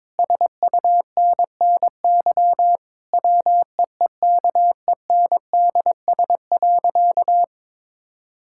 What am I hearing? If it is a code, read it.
Morse code, heard as a keyed tone, about 22 wpm, 700 Hz. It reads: SUNNY WEEKENDS.
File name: Difficult Sentences with Hints - 22wpm